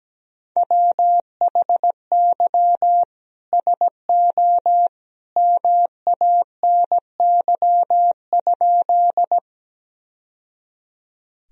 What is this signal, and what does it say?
Morse code, heard as a keyed tone, about 17 wpm, 700 Hz. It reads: WHY SO MANY?
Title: Sentences from Top 1000 Words - Review Part 1 - 17wpm